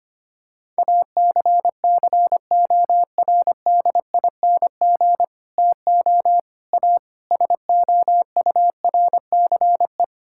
Morse code, heard as a keyed tone, about 25 wpm, 700 Hz. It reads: ACCORDING TO A SOURCE
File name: Sets of 9 Words - 25wpm